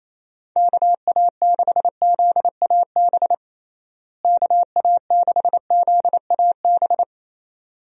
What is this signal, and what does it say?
Morse code, heard as a keyed tone, about 28 wpm, 700 Hz. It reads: KA6ZAB KA6ZAB
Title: USA Call Signs - 6 Characters - Repeated 2x - 28wpm